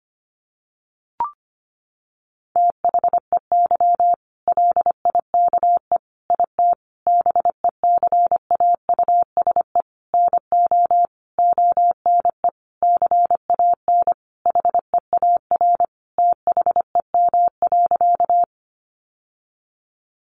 Morse code, heard as a keyed tone, about 25 wpm, 700 Hz. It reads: THEY LIKE IT BECAUSE NO ONE CAN HEAR THEM.
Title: Sentences from Top 600 Words 25wpm